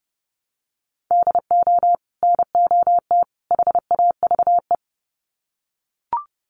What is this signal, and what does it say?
Morse code, heard as a keyed tone, about 30 wpm, 700 Hz. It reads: DO NOT HAVE
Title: Sets of 3 Words 30wpm